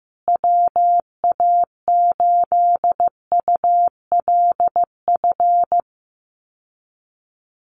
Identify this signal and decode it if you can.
Morse code, heard as a keyed tone, about 15 wpm, 700 Hz. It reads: WA8ULF
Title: USA Call Signs Repeated 2x 15wpm